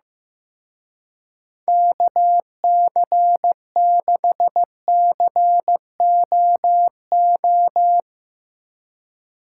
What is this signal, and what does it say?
Morse code, heard as a keyed tone, about 16 wpm, 700 Hz. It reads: KC6COO
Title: Speed Racing - USA 2x3 Call Signs 15wpm